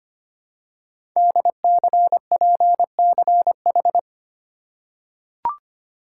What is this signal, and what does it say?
Morse code, heard as a keyed tone, about 25 wpm, 700 Hz. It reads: DCPCH
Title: Code Groups - 5 Letters - 25wpm